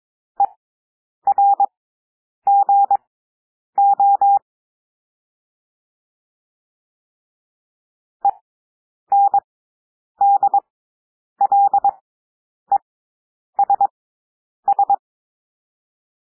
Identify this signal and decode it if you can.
Morse code, heard as a keyed tone, about 22 wpm, spaced as for 9 wpm, 800 Hz. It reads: ERGO ENDLESS
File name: cwmp3 / W2 SOUND WORDS 22 wpm